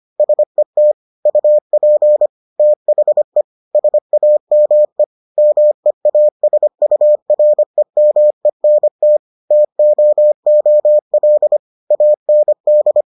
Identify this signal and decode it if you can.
Morse code, heard as a keyed tone, about 25 wpm, 600 Hz. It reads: SET UP THE SAME MEASUREMENT TOOL AND